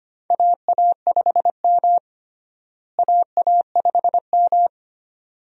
Morse code, heard as a keyed tone, about 25 wpm, 700 Hz. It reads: AA5M AA5M
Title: USA Call Signs - 4 Characters - Repeated 2x - 25wpm